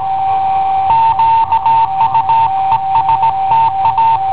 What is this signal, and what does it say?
Morse code, heard as a keyed tone, about 18 wpm, 910 Hz. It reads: QUESTA
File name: i7ohp / cassetta i7ohp B